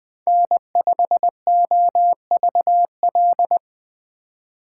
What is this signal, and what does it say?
Morse code, heard as a keyed tone, about 20 wpm, 700 Hz. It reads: N5OVL
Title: Speed Racing - USA Call Signs 17wpm